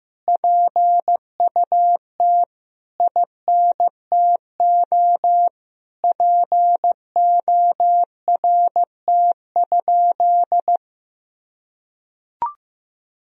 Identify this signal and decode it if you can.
Morse code, heard as a keyed tone, about 15 wpm, 700 Hz. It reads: PUT INTO PORT?
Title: Sentences from Top 1000 Words - Review Part 1 - 15wpm